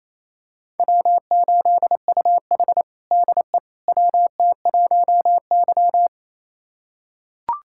Morse code, heard as a keyed tone, about 28 wpm, 700 Hz. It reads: W8UH DE WT1Y
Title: USA Callsign DE Callsign - 4 Characters - 28wpm